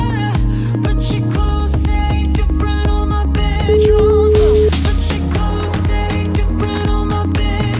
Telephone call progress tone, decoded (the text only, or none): ringback tone